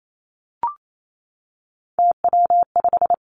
Morse code, heard as a keyed tone, about 28 wpm, 700 Hz. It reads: TW5
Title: Three Challenging Letter-Number Combinations 28wpm